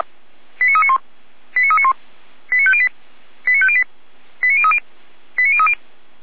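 Five-tone selective call (ZVEI), heard as quick runs of five tones, tones of about 70 ms. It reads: 89391 89391 89508 89508 89030 89030
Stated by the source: ZVEI five-tone selective call received by SDR.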